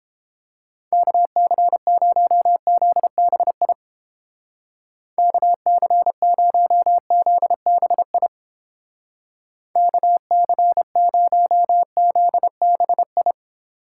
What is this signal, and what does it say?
Morse code, heard as a keyed tone, about 30 wpm, 700 Hz. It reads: KC0ZBI KC0ZBI KC0ZBI